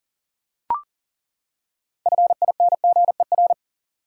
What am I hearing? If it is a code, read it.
Morse code, heard as a keyed tone, about 40 wpm, 700 Hz. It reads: FINGER
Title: Six Letter Words - 40wpm